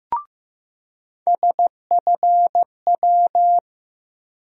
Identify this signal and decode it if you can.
Morse code, heard as a keyed tone, about 15 wpm, 700 Hz. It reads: SFW
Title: USA Call Sign Postfixes - 3 Letters 15wpm